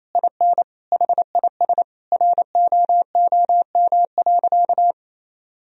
Morse code, heard as a keyed tone, about 28 wpm, 700 Hz. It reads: IN HIS ROOM.
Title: Sentences from Top 1000 Words - Review Part 4 - 28wpm